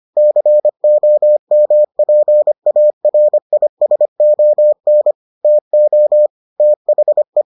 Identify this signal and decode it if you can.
Morse code, heard as a keyed tone, about 25 wpm, 600 Hz. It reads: COMPARISON TO THE